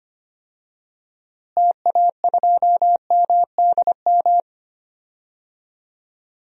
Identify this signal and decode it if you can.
Morse code, heard as a keyed tone, about 25 wpm, 700 Hz. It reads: TA2MDM
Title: DXCC-100 Call Signs - 6 Characters - Repeated 2x - 25wpm